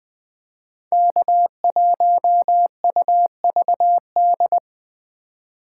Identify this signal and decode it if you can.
Morse code, heard as a keyed tone, about 21 wpm, 700 Hz. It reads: K1UVD